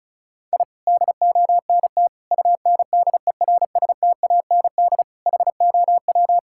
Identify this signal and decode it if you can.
Morse code, heard as a keyed tone, about 34 wpm, 700 Hz. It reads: I DONT UNDERSTAND HOW
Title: Sentences from Top 1000 Words - Review Part 5 - 35wpm